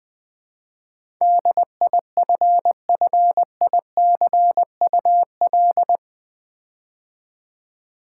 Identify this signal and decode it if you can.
Morse code, heard as a keyed tone, about 20 wpm, 700 Hz. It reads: DIFFICUL
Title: Build A Word - 4 Letters At A Time - 10-Letter Words - 20wpm